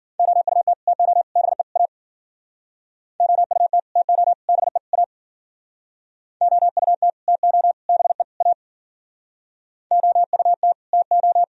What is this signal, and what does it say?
Morse code, heard as a keyed tone, about 51 wpm, 700 Hz. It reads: OUT TO BE A OUT TO BE A OUT TO BE A OUT TO